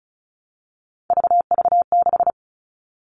Morse code, heard as a keyed tone, about 34 wpm, 700 Hz. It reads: VV6
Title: All Cut Numbers - 3-Digit 35wpm